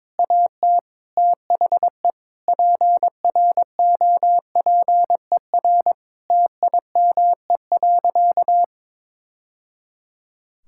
Morse code, heard as a keyed tone, about 22 wpm, 700 Hz. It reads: AT THE PROPER TIME.
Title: Sentences from Top 1000 Words - Review Part 5 - 22wpm